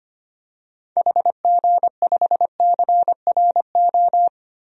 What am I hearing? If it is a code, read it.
Morse code, heard as a keyed tone, about 25 wpm, 700 Hz. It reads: HG5CRO
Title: DXCC-100 Call Signs - 6 Characters - Repeated 3x - 25wpm